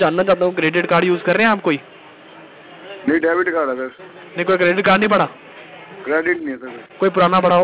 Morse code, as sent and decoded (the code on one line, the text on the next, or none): none